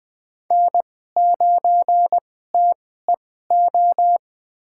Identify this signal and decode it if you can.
Morse code, heard as a keyed tone, about 21 wpm, spaced as for 15 wpm, 700 Hz. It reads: N9TEO
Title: USA Call Signs - 2x Character Spacing - 20wpm